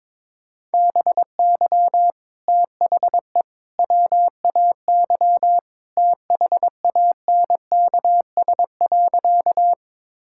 Morse code, heard as a keyed tone, about 21 wpm, 700 Hz. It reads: BY THE WAY THANKS.
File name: practice / 22wpm / Sentences with 3-Word Phrases - 22wpm